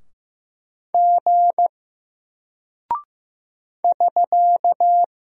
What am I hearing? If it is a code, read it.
Morse code, heard as a keyed tone, about 15 wpm, 700 Hz. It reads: G <SK>
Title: Single Character (Letter:Number:Punctuation:Prosign) 15wpm